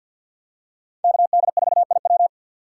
Morse code, heard as a keyed tone, about 49 wpm, 700 Hz. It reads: KD4IW
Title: USA Call Signs - 5 Characters - 4x Character Spacing - 50wpm